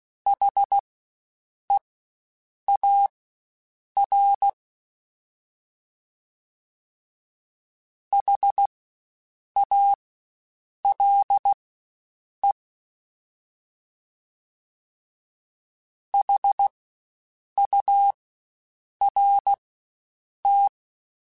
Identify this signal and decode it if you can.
Morse code, heard as a keyed tone, about 16 wpm, spaced as for 7 wpm, 800 Hz. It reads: HEAR HALE HURT
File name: Spaced Alphas / Spaced Alphas mp3s / Hs sp @15